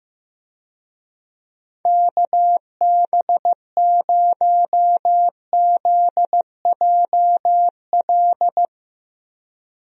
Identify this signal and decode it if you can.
Morse code, heard as a keyed tone, about 16 wpm, 700 Hz. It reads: KB0ZJL
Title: USA Call Signs - 6 Characters - Repeated 2x - 15wpm